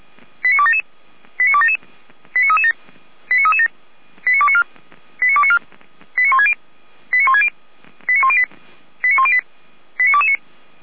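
Five-tone selective call (ZVEI), heard as quick runs of five tones, tones of about 70 ms. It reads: 89277 89277 89337 89337 89294 89294 89160 89160 89108 89108 89229
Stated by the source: ZVEI five-tone call received on an SDR.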